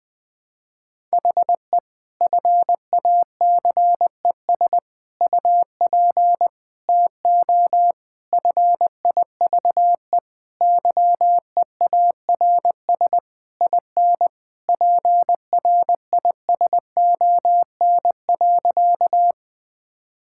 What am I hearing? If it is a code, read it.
Morse code, heard as a keyed tone, about 19 wpm, 700 Hz. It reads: HE FACES UP TO FIVE YEARS IN PRISON.
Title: Sentences with 6-Word Phrases - Spoken Beginning - 20wpm